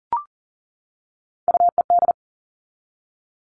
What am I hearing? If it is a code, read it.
Morse code, heard as a keyed tone, about 40 wpm, 700 Hz. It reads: UED
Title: All Cut Numbers - 3-Digit 40wpm